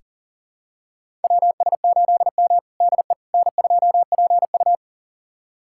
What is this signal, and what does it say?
Morse code, heard as a keyed tone, about 40 wpm, 700 Hz. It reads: WS8M DE N2PU
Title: USA Callsign DE Callsign - 4 Characters - 40wpm